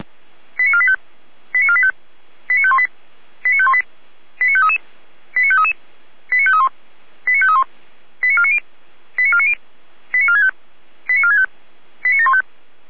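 Five-tone selective call (ZVEI), heard as quick runs of five tones, tones of about 70 ms. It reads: 89485 89485 89518 89518 89533 89533 89531 89531 89490 89490 89465 89465 89715
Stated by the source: Five-tone ZVEI selective call received on SDR.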